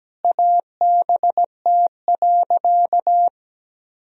Morse code, heard as a keyed tone, about 17 wpm, 700 Hz. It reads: ABT.
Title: Sentences from Top 400 Words and Abbreviations 17wpm